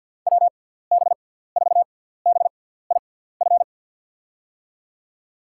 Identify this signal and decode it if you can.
Morse code, heard as a keyed tone, about 48 wpm, spaced as for 17 wpm, 700 Hz. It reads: WB4BIF
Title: USA Call Signs - 6x Character Spacing - 50wpm